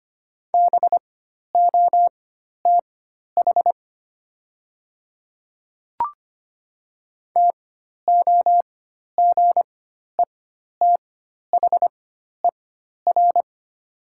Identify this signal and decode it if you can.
Morse code, heard as a keyed tone, about 25 wpm, spaced as for 12 wpm, 700 Hz. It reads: BOTH TOGETHER
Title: 4x Character Spacing - Top 300 Words - 25wpm